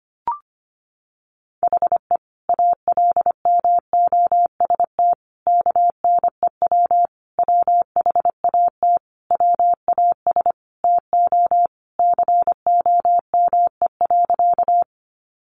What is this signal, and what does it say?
Morse code, heard as a keyed tone, about 25 wpm, 700 Hz. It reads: HE ALMOST KNEW WHAT WAS TO COME.
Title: Sentences from Top 700 Words 25wpm